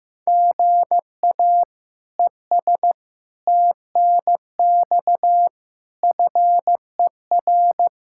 Morse code, heard as a keyed tone, about 15 wpm, 700 Hz. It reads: GA ES TNX FER
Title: Practice QSO - Exchange 1 - Beginning - 15wpm